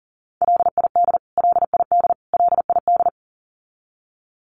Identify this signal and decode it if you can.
Morse code, heard as a keyed tone, about 40 wpm, 700 Hz. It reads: LID LID LID
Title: Abbreviations Repeated 3x - 40wpm